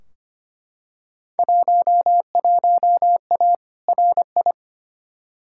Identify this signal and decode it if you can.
Morse code, heard as a keyed tone, about 25 wpm, 700 Hz. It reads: 11A RI
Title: Field Day Exchanges Encore - 25wpm